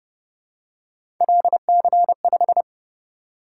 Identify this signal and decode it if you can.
Morse code, heard as a keyed tone, about 29 wpm, 700 Hz. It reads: LC5
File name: Speed-Racing - Three Letter-Number Combinations - 25wpm